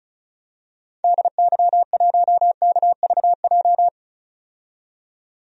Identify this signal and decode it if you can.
Morse code, heard as a keyed tone, about 36 wpm, 700 Hz. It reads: DY1KVJ
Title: DXCC-100 Call Signs - 6 Characters - 35wpm